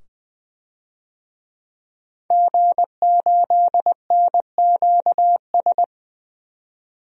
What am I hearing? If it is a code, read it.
Morse code, heard as a keyed tone, about 20 wpm, 700 Hz. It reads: G8NQS
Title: DXCC-100 Call Signs - 5 Characters - 20wpm